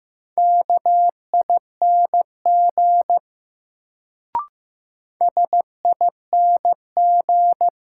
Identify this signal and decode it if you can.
Morse code, heard as a keyed tone, about 15 wpm, 700 Hz. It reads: KING SING
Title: Sentences from Top 400 Words - New Word Drill - 15wpm